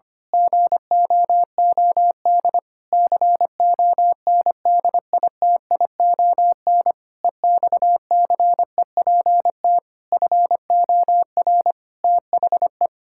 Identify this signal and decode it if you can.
Morse code, heard as a keyed tone, about 25 wpm, 700 Hz. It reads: GOOD CONDITION EXCEPT FOR THE
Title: Pangram Challenge v2 - 25wpm